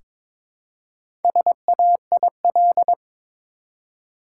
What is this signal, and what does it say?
Morse code, heard as a keyed tone, about 22 wpm, 700 Hz. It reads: SAIL
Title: Four Letter Words - 22wpm